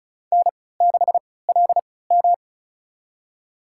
Morse code, heard as a keyed tone, about 34 wpm, spaced as for 20 wpm, 700 Hz. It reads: N6LM
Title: USA Call Signs - 4 Characters - 3x Character Spacing - 35wpm